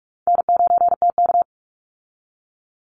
Morse code, heard as a keyed tone, about 44 wpm, 700 Hz. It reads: N9TK
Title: Easy Call Signs Repeated 3x - 45wpm